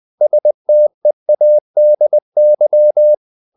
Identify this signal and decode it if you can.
Morse code, heard as a keyed tone, about 20 wpm, 600 Hz. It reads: STEADY
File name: the old man and the sea / the old man and the sea part 1 20wpm